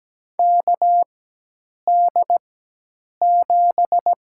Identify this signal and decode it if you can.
Morse code, heard as a keyed tone, about 17 wpm, spaced as for 8 wpm, 700 Hz. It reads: KD7
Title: USA Call Signs - 4x Character Spacing - 17wpm